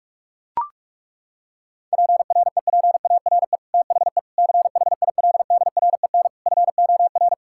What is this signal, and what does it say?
Morse code, heard as a keyed tone, about 41 wpm, 700 Hz. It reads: PREPARE THE CHILDREN FOR
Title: Sentences from Top 1000 Words - Review Part 6 - 45wpm